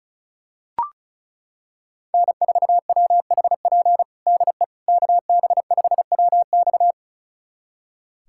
Morse code, heard as a keyed tone, about 36 wpm, 700 Hz. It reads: N4WHP DE KB5WX
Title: USA Callsign DE Callsign - 5 Characters - 35wpm